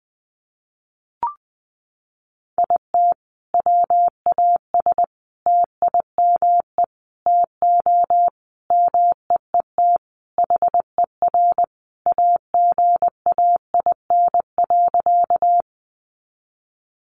Morse code, heard as a keyed tone, about 20 wpm, 700 Hz. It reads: IT WAS TIME TO MEET HER AGAIN.
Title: Sentences from Top 1100 Words 20wpm